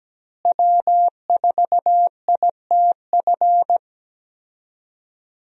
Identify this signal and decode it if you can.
Morse code, heard as a keyed tone, about 17 wpm, 700 Hz. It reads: W4ITF
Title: USA Call Signs - 5 Characters - Repeated 3x - 17wpm